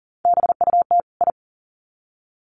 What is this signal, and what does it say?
Morse code, heard as a keyed tone, about 40 wpm, 700 Hz. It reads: BUT I
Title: Sets of 2 Words 40wpm